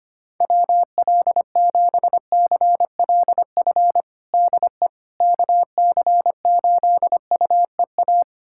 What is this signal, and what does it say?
Morse code, heard as a keyed tone, about 25 wpm, 700 Hz. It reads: WL7CLF DE KC8UEA
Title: USA Callsign DE Callsign - 6 Characters - 25wpm